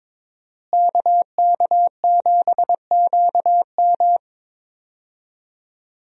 Morse code, heard as a keyed tone, about 22 wpm, 700 Hz. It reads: KK7QM
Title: USA Call Signs - 5 Characters - Repeated 3x - 22wpm